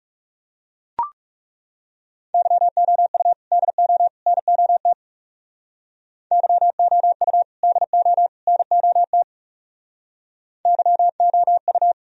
Morse code, heard as a keyed tone, about 38 wpm, 700 Hz. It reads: YOU DO NOT YOU DO NOT YOU